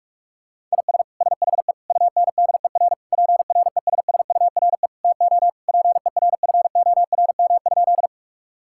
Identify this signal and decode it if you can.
Morse code, heard as a keyed tone, about 41 wpm, 700 Hz. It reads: IS SHE UNDER PRESSURE TO PERFORM?
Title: Sentences from Top 1000 Words - Review Part 6 - 45wpm